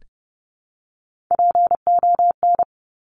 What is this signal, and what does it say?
Morse code, heard as a keyed tone, about 30 wpm, 700 Hz. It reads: PON
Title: Top Three Letter Combinations in Words 30wpm